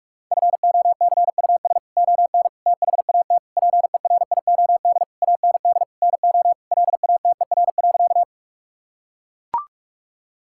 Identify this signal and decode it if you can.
Morse code, heard as a keyed tone, about 41 wpm, 700 Hz. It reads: FOCUS ON THAT PERIOD AND NO LATER.
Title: Sentences from Top 1000 Words - Review Part 5 - 45wpm